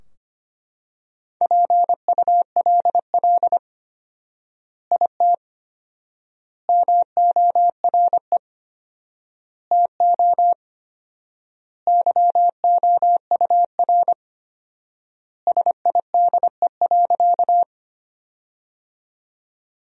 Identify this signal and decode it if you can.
Morse code, heard as a keyed tone, about 25 wpm, 700 Hz. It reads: PULL IT MORE TO YOUR SIDE.
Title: 4x Word-Spacing - Sentences from Top 400 Words - 25wpm